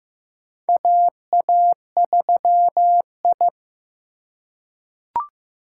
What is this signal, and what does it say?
Morse code, heard as a keyed tone, about 15 wpm, 700 Hz. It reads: AA3I
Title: USA Call Signs - 4 Characters - 4x Character Spacing - 15wpm